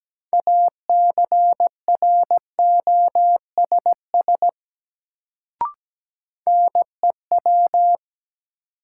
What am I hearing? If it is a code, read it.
Morse code, heard as a keyed tone, about 17 wpm, 700 Hz. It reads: ACROSS NEW
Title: Sets of 2 Words - New Word Drill - 17wpm